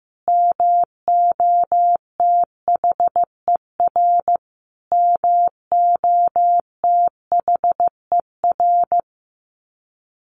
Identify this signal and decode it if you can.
Morse code, heard as a keyed tone, about 15 wpm, 700 Hz. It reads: MOTHER MOTHER
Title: Top 200 Words Repeated 15wpm